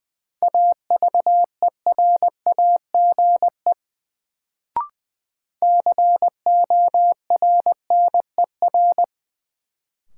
Morse code, heard as a keyed tone, about 20 wpm, 700 Hz. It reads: AVERAGE CORNER
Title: Sets of 5 Words - New Word Drill - 20wpm